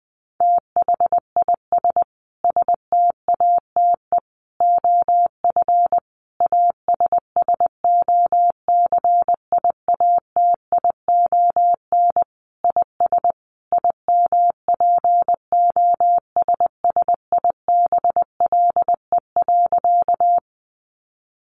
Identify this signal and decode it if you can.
Morse code, heard as a keyed tone, about 20 wpm, 700 Hz. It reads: THIS STATE OF ASSOCIATION IS IMPOSSIBLE.
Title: Sentences from Top 1100 Words 20wpm